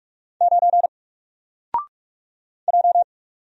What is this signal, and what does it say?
Morse code, heard as a keyed tone, about 44 wpm, 700 Hz. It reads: 9 J
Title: Single Letter-Number Alt 45wpm